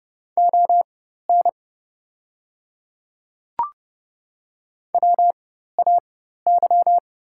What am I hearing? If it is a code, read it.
Morse code, heard as a keyed tone, about 30 wpm, spaced as for 14 wpm, 700 Hz. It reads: ON WAY
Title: 4x Character Spacing - Top 200 Words - 30wpm